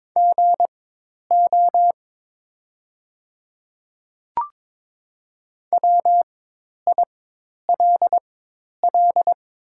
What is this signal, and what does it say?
Morse code, heard as a keyed tone, about 22 wpm, spaced as for 10 wpm, 700 Hz. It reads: GO WILL
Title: 4x Character Spacing - Top 100 Words - 22wpm